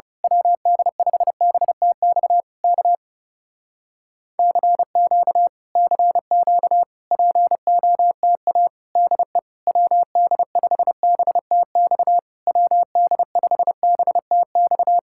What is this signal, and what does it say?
Morse code, heard as a keyed tone, about 28 wpm, 700 Hz. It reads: WD5BTX K CQ CQ POTA DE WD5BTX WD5BTX
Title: Speed-Racing Common QSO Phrases - 30wpm